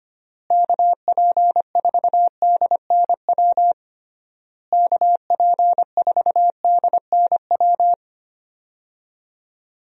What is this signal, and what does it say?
Morse code, heard as a keyed tone, about 25 wpm, 700 Hz. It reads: KP4DNW KP4DNW
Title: USA Call Signs - 6 Characters - Repeated 3x - 25wpm